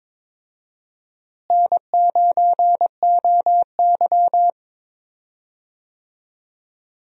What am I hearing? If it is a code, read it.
Morse code, heard as a keyed tone, about 21 wpm, 700 Hz. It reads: N9OY
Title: USA Call Signs - 4 Characters - Repeated 2x - 22wpm